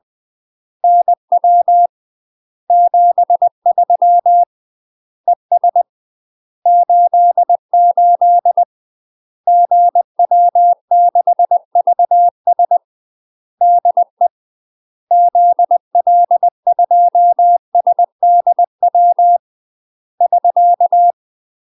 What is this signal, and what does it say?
Morse code, heard as a keyed tone, about 20 wpm, 700 Hz. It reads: NW 73 ES 88 GW6VS DE ZL2SDW <SK>
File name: UK MASTER QSOs TRAINING / 20WPM / UK MASTER QSOs 0002